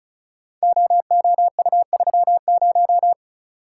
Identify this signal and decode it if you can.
Morse code, heard as a keyed tone, about 36 wpm, 700 Hz. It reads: OOU30
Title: Code Groups - 5 Letter:Numbers - 35wpm